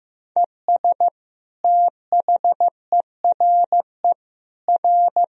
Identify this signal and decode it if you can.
Morse code, heard as a keyed tone, about 15 wpm, 700 Hz. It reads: ES THERE R
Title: Sentences from Top 300 Words and Abbreviations 15wpm